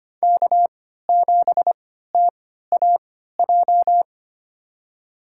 Morse code, heard as a keyed tone, about 25 wpm, spaced as for 14 wpm, 700 Hz. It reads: K7TAJ
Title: USA Call Signs - 5 Characters - 3x Character Spacing - 25wpm